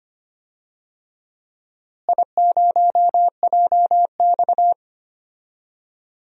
Morse code, heard as a keyed tone, about 25 wpm, 700 Hz. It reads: I0JX
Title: DXCC-100 Call Signs - 4 Characters - Repeated 3x - 25wpm